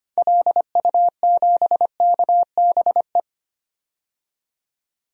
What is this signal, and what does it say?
Morse code, heard as a keyed tone, about 25 wpm, 700 Hz. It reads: LU7KBE
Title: DXCC-100 Call Signs - 6 Characters - 25wpm